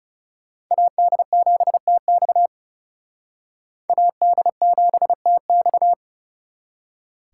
Morse code, heard as a keyed tone, about 39 wpm, 700 Hz. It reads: AD7TX AD7TX